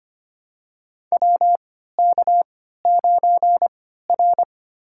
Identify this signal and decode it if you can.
Morse code, heard as a keyed tone, about 25 wpm, spaced as for 14 wpm, 700 Hz. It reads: WK9R
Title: USA Call Signs - 4 Characters - 3x Character Spacing - 25wpm